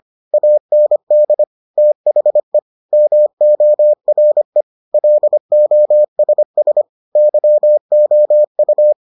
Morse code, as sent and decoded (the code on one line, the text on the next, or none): .- -. -.. / - .... / -- --- .-. / .-.. --- ... ... / -.-- --- ..-
AND THE MORE LOSS YOU